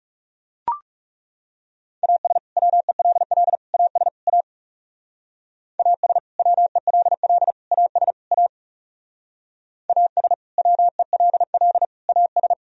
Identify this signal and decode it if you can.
Morse code, heard as a keyed tone, about 37 wpm, 690 Hz. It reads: AS WELL AS A AS WELL AS A AS WELL AS
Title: Speed-Racing - Sets of 4 Words 30wpm